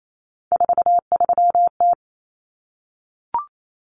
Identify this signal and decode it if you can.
Morse code, heard as a keyed tone, about 28 wpm, 700 Hz. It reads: 43T
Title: Common Cut Numbers - 3-Digit 28wpm